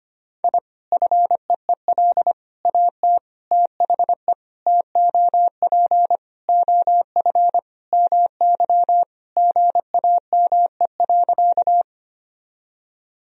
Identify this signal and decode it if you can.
Morse code, heard as a keyed tone, about 25 wpm, 700 Hz. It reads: I FEEL AT THE TOP OF MY GAME.
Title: Sentences from Top 1000 Words - Review Part 2 - 25wpm